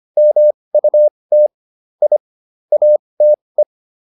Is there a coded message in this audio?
Yes: Morse code from 0.2 to 3.6 s, heard as a keyed tone, about 25 wpm, spaced as for 20 wpm, 600 Hz. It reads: MUT I ATE